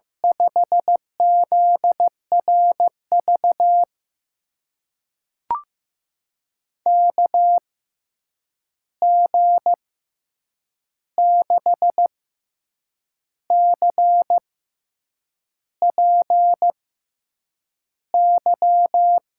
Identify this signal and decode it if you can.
Morse code, heard as a keyed tone, about 15 wpm, spaced as for 7 wpm, 700 Hz. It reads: 5ZRV KG6CPY
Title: USA Call Signs - 6 Characters - 6x Character Spacing - 15wpm